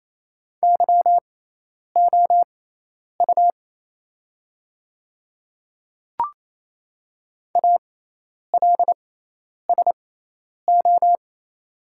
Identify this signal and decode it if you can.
Morse code, heard as a keyed tone, about 27 wpm, spaced as for 10 wpm, 700 Hz. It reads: YOU ALSO
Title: 6x Character Spacing - Top 300 Words - 28wpm